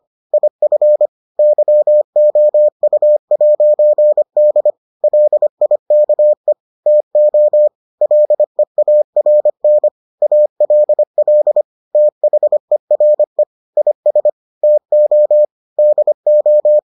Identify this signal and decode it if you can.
Morse code, heard as a keyed tone, about 25 wpm, 600 Hz. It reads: IF YOU'D LIKE TO LEARN ALL THERE IS TO DO